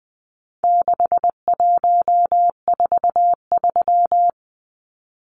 Morse code, heard as a keyed tone, about 20 wpm, 700 Hz. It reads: 6143
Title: Numbers - 4-Digit 20wpm